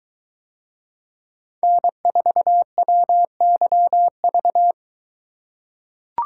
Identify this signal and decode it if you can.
Morse code, heard as a keyed tone, about 23 wpm, 700 Hz. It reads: N4WYV